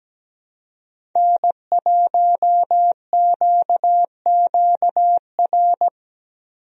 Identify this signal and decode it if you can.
Morse code, heard as a keyed tone, about 17 wpm, 700 Hz. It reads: N1QQR
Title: USA Call Signs - 5 Characters - Repeated 2x - 17wpm